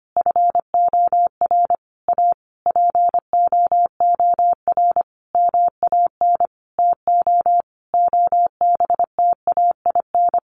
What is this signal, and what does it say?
Morse code, heard as a keyed tone, about 25 wpm, 700 Hz. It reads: FOR A POOR MAN TO OBTAIN
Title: Sentences from Top 1300 Words 25wpm